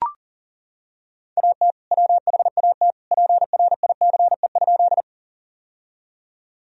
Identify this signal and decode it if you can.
Morse code, heard as a keyed tone, about 40 wpm, 700 Hz. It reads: AT WHAT PRICE?
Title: Sentences from Top 1000 Words - Review Part 4 - 40wpm